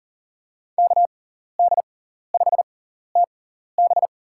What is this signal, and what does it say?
Morse code, heard as a keyed tone, about 38 wpm, spaced as for 14 wpm, 700 Hz. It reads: KD5TB
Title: USA Call Signs - 5 Characters - 6x Character Spacing - 40wpm